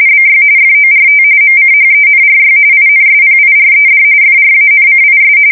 RTTY, about 45.45 baud. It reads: BY FORT MCHENRY INSPIRED FRANCIS
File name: fsk170 / usnavy2